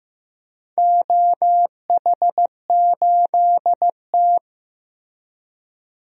Morse code, heard as a keyed tone, about 16 wpm, 700 Hz. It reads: OH8T
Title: DXCC-100 Call Signs - 4 Characters - 15wpm